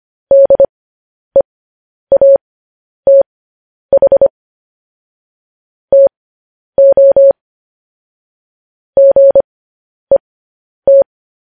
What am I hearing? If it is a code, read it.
Morse code, heard as a keyed tone, about 25 wpm, spaced as for 10 wpm, 550 Hz. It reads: DEATH TO GET
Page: news-headlines-in-morse-code-at-10-wpm